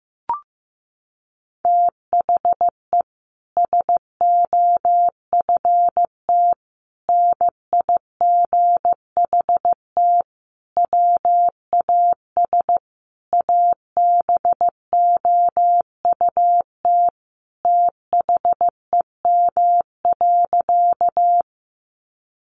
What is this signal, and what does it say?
Morse code, heard as a keyed tone, about 15 wpm, 700 Hz. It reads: THE SOFT NIGHT WAS ABOUT THEM.
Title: Sentences from Top 1500 Words 15wpm